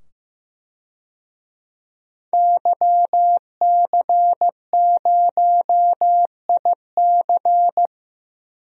Morse code, heard as a keyed tone, about 15 wpm, 700 Hz. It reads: YC0IC
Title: DXCC-100 Call Signs - 5 Characters - 15wpm